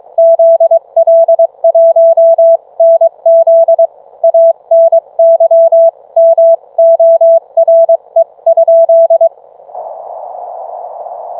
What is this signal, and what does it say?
Morse code, heard as a keyed tone, about 22 wpm, 660 Hz. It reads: ZL1NZ ANYMORE?